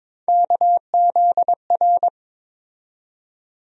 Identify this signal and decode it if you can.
Morse code, heard as a keyed tone, about 22 wpm, 700 Hz. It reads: KZR
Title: USA Call Sign Postfixes - 3 Letters 22wpm